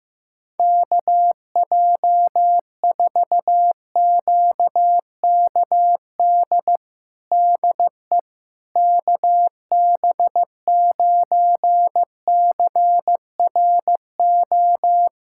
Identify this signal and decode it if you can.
Morse code, heard as a keyed tone, about 16 wpm, 700 Hz. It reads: KJ4QKD DE KB9CRO
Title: USA Callsign DE Callsign - 6 Characters - 15wpm